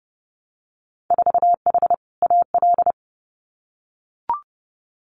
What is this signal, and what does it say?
Morse code, heard as a keyed tone, about 30 wpm, 700 Hz. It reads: 4H AL
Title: Winter Field Day Exchanges Encore 30wpm